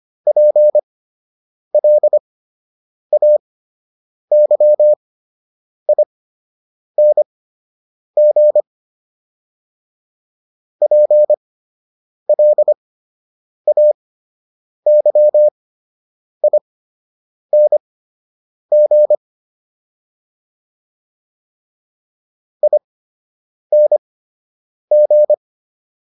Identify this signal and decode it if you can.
Morse code, heard as a keyed tone, about 25 wpm, spaced as for 8 wpm, 600 Hz. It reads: PLAYING PLAYING ING